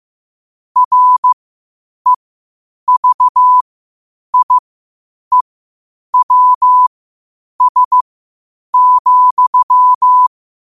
Morse code, heard as a keyed tone, about 15 wpm, spaced as for 8 wpm, 1000 Hz. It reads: REVIEWS,
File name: qotdfiles / 8wpm